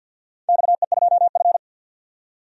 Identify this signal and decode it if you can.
Morse code, heard as a keyed tone, about 46 wpm, 700 Hz. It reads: XE2F
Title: DXCC-100 Call Signs - 4 Characters - 50wpm